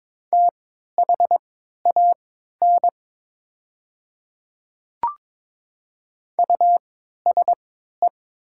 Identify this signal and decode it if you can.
Morse code, heard as a keyed tone, about 22 wpm, spaced as for 12 wpm, 700 Hz. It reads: THAN USE
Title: 3x Character Spacing - Top 300 Words - 22wpm